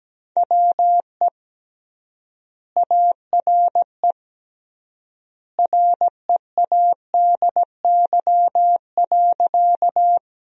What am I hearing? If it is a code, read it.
Morse code, heard as a keyed tone, about 17 wpm, 700 Hz. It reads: WE ARE READY.
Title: 3x Word-Spacing - Sentences from Top 400 Words - 17wpm